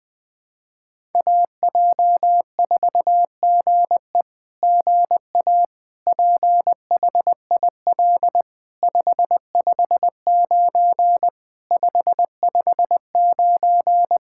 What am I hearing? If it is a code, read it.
Morse code, heard as a keyed tone, about 20 wpm, 690 Hz. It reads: AJ4GE GA PHIL 559 559
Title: SOTA QSO Practice 20wpm